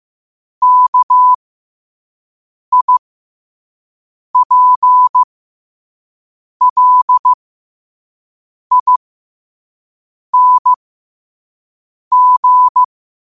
Morse code, heard as a keyed tone, about 15 wpm, spaced as for 5 wpm, 1000 Hz. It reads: KIPLING
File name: qotdfiles / 5wpm